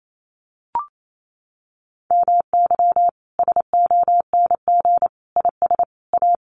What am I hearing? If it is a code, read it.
Morse code, heard as a keyed tone, about 28 wpm, 700 Hz. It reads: MY SONG IS A
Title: Sentences from Top 1500 Words 28wpm